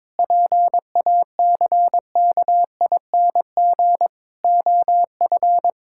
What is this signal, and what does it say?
Morse code, heard as a keyed tone, about 21 wpm, 700 Hz. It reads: PACKING OF
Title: Pangram Challenge v2 - 22wpm